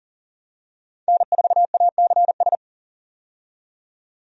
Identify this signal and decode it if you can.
Morse code, heard as a keyed tone, about 40 wpm, 700 Hz. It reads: N4ACS
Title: USA Call Signs - 5 Characters - 40wpm